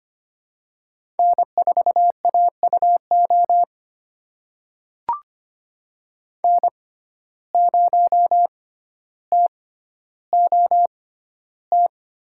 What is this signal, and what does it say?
Morse code, heard as a keyed tone, about 25 wpm, spaced as for 13 wpm, 700 Hz. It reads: N4AUO N0TOT
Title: USA Call Signs - 5 Characters - 6x Character Spacing - 25wpm